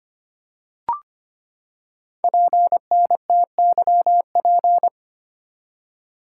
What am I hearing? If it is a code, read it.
Morse code, heard as a keyed tone, about 25 wpm, 700 Hz. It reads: PNTYP
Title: Code Groups - 5 Letter:Numbers - 25wpm